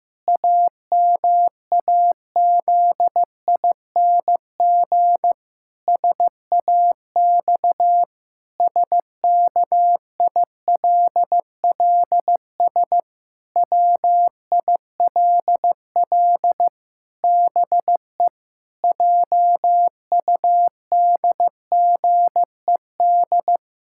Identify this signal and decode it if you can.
Morse code, heard as a keyed tone, about 16 wpm, 700 Hz. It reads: AMAZING SAX SKILLS WILL BE JUDGED
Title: Pangram Challenge v2 - 15wpm